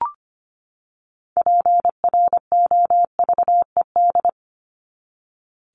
Morse code, heard as a keyed tone, about 25 wpm, 700 Hz. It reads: PROVED
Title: ED Suffix - Top 100 Words with Base Words 25wpm